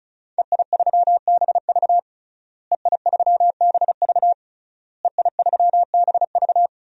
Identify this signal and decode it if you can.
Morse code, heard as a keyed tone, about 36 wpm, 700 Hz. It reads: EI3BV EI3BV EI3BV
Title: DXCC-100 Call Signs - 5 Characters - Repeated 3x - 35wpm